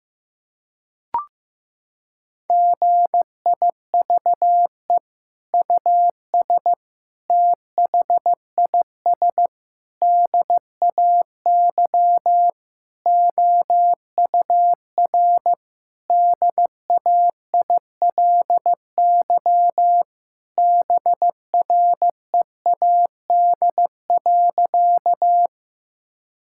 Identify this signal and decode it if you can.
Morse code, heard as a keyed tone, about 16 wpm, 700 Hz. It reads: GIVE US THIS DAY OUR DAILY BREAD.
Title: Sentences from Top 1600 Words 15wpm